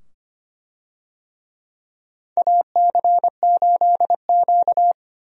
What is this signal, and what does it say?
Morse code, heard as a keyed tone, about 25 wpm, 700 Hz. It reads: AC8Q
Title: USA Call Signs - 4 Characters - 25wpm